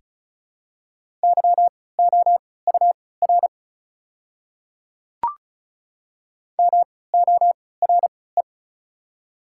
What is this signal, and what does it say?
Morse code, heard as a keyed tone, about 36 wpm, spaced as for 20 wpm, 690 Hz. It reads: YOUR MORE
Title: 3x Character Spacing - Top 200 Words - 35wpm